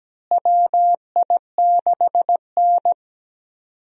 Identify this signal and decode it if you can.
Morse code, heard as a keyed tone, about 17 wpm, 700 Hz. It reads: WI6N
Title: USA Call Signs - 4 Characters - Repeated 3x - 17wpm